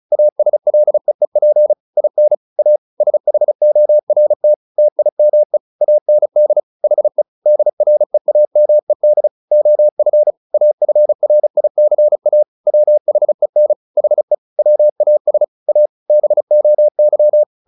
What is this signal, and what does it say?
Morse code, heard as a keyed tone, about 34 wpm, 600 Hz. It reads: ASLEEP IN A SHORT TIME AND HE DREAMED OF AFRICA WHEN HE WAS A BOY
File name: the old man and the sea / the old man and the sea 1 35wpm